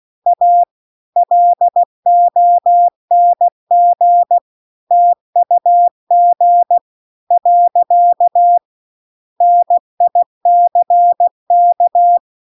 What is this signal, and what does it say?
Morse code, heard as a keyed tone, about 16 wpm, 700 Hz. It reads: A LONG TUG . NICK